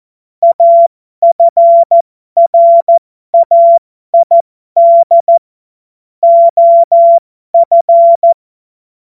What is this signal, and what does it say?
Morse code, heard as a keyed tone, about 14 wpm, spaced as for 12 wpm, 680 Hz. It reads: AFRAID OF
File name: Morsekurs / Witze